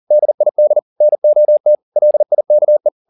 Morse code, heard as a keyed tone, about 40 wpm, 600 Hz. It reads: DID NOT LIKE
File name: the old man and the sea 2 40wpm